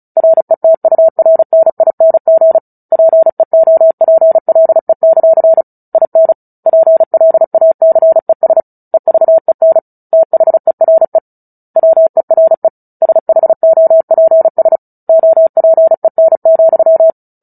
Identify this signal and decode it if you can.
Morse code, heard as a keyed tone, about 34 wpm, 650 Hz. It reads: RETURNING PEOPLE; IN PLACES EVEN THERE WERE SHOPS OPEN,
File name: Book Two / 09-Chapter 9 - 35 WPM